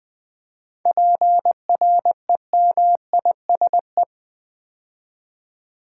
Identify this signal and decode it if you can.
Morse code, heard as a keyed tone, about 20 wpm, 700 Hz. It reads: PREMISE
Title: 7 Letter Words - First 3 Letters Given - 20wpm